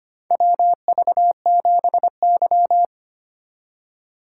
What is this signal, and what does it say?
Morse code, heard as a keyed tone, about 25 wpm, 700 Hz. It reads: WV7Y
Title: USA Call Signs - 4 Characters - 4x Character Spacing - 25wpm